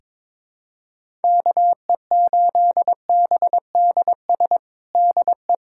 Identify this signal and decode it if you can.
Morse code, heard as a keyed tone, about 22 wpm, 700 Hz. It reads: KE8BDS DE
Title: USA Callsign DE Callsign 22wpm